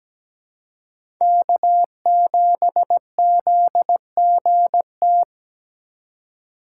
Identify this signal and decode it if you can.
Morse code, heard as a keyed tone, about 17 wpm, 700 Hz. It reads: K7ZGT
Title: USA Call Signs - 5 Characters - 4x Character Spacing - 17wpm